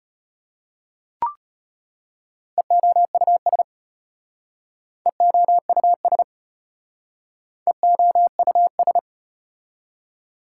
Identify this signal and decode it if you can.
Morse code, heard as a keyed tone, about 35 wpm, 700 Hz. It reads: EOUS EOUS EOUS